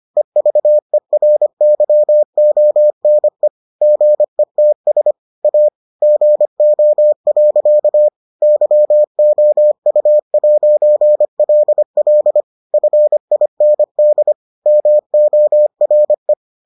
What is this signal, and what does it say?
Morse code, heard as a keyed tone, about 25 wpm, 600 Hz. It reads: EVERYONE GETS A GO. YOU'LL FIND MORE